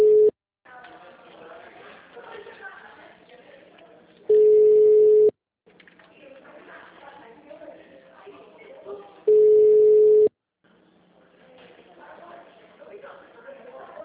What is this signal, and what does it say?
Ringback tone (Europe, 425 Hz), with speech.